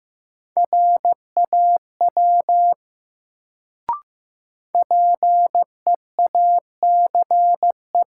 Morse code, heard as a keyed tone, about 15 wpm, 700 Hz. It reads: RAW PEACE
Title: Sentences from Top 1500 Words - New Word Drill - 15wpm